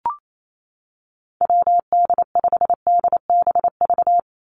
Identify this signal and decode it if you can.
Morse code, heard as a keyed tone, about 28 wpm, 700 Hz. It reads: WD5DBV
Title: Call Signs Repeated 2x - 28wpm